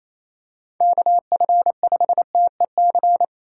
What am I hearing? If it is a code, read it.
Morse code, heard as a keyed tone, about 27 wpm, 700 Hz. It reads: KF5TEC
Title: USA Call Signs - 6 Characters - 4x Character Spacing - 28wpm